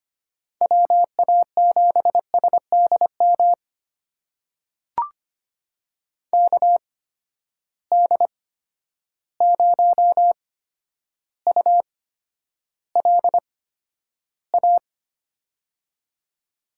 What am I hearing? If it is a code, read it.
Morse code, heard as a keyed tone, about 25 wpm, spaced as for 11 wpm, 700 Hz. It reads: WA7SDM KD0ULA
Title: USA Call Signs - 6 Characters - 8x Character Spacing - 25wpm